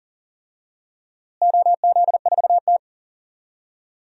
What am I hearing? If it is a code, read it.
Morse code, heard as a keyed tone, about 40 wpm, 700 Hz. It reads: OZ4T
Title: DXCC-100 Call Signs - 4 Characters - Repeated 2x - 40wpm